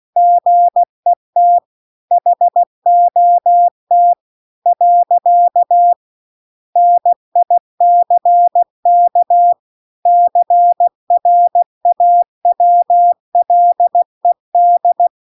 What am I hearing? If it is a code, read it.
Morse code, heard as a keyed tone, about 16 wpm, 700 Hz. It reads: GET HOT . NICK CRAWLED
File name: 16WPM